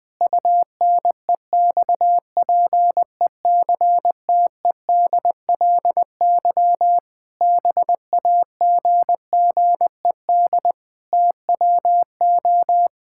Morse code, heard as a keyed tone, about 21 wpm, 700 Hz. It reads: UNEXPECTEDLY BAGGED TWO
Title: Pangram Challenge v2 - 20wpm